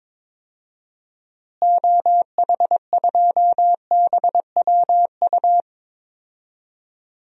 Morse code, heard as a keyed tone, about 22 wpm, 700 Hz. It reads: OH2BWU
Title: DXCC-100 Call Signs - 6 Characters - Repeated 2x - 22wpm